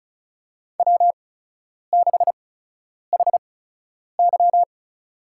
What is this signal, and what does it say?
Morse code, heard as a keyed tone, about 34 wpm, spaced as for 10 wpm, 700 Hz. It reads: W6HY
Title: USA Call Signs - 4 Characters - 8x Character Spacing - 35wpm